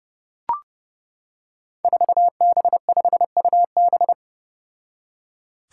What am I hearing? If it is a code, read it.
Morse code, heard as a keyed tone, about 30 wpm, 700 Hz. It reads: 4B5UB
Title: Five Challenging Letter-Number Combinations 30wpm